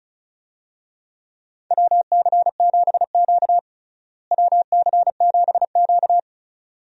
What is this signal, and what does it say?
Morse code, heard as a keyed tone, about 36 wpm, 700 Hz. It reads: WC7Q WC7Q
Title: USA Call Signs - 4 Characters - Repeated 2x - 35wpm